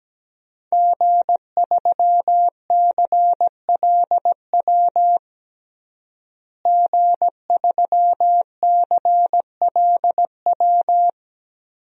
Morse code, heard as a keyed tone, about 17 wpm, 700 Hz. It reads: G3CLW G3CLW
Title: DXCC-100 Call Signs - 5 Characters - Repeated 3x - 17wpm